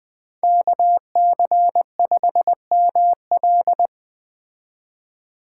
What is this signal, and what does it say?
Morse code, heard as a keyed tone, about 20 wpm, 700 Hz. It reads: KC5ML
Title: Speed Racing - USA Call Signs 17wpm